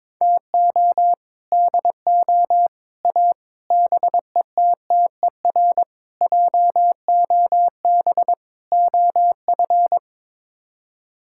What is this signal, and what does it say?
Morse code, heard as a keyed tone, about 22 wpm, 700 Hz. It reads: TO DO A BETTER JOB OF
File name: practice / 22wpm / Sets of 6 Words - 22wpm